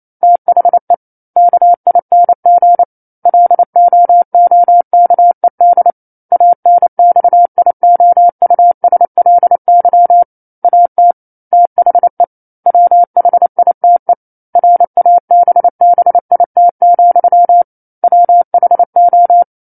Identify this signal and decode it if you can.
Morse code, heard as a keyed tone, about 28 wpm, 700 Hz. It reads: THE KING LOOKED ANXIOUSLY AT THE WHITE RABBIT, WHO
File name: 09-Chapter-09 28 WPM